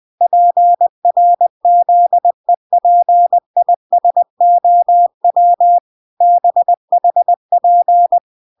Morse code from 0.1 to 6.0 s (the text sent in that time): PRZEPISOW